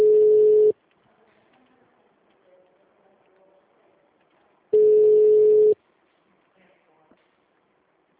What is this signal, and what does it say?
Ringback tone (Europe, 425 Hz).
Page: Dracena 2 - SP